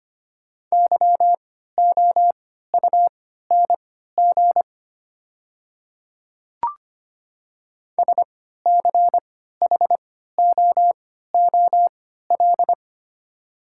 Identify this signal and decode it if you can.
Morse code, heard as a keyed tone, about 25 wpm, spaced as for 14 wpm, 700 Hz. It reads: YOUNG SCHOOL
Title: 3x Character Spacing - Top 500 Words - 25wpm